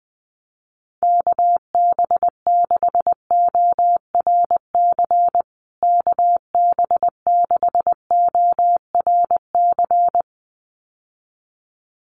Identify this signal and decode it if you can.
Morse code, heard as a keyed tone, about 20 wpm, 700 Hz. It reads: KB6ORC KB6ORC
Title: USA Call Signs Repeated 2x 20wpm